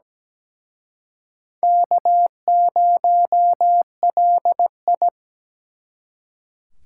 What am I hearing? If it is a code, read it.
Morse code, heard as a keyed tone, about 20 wpm, 700 Hz. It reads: K0LI
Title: Speed Racing - USA 1x2 Call Signs 17wpm